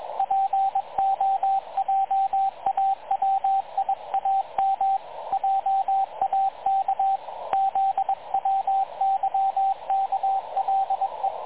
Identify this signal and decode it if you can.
Morse code, heard as a keyed tone, about 23 wpm, 780 Hz, with radio noise. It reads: POJAWIAM JAK ZWYKLE
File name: dlazaawansowanych / nagraniazpasma / websdr recording 3567.0kHz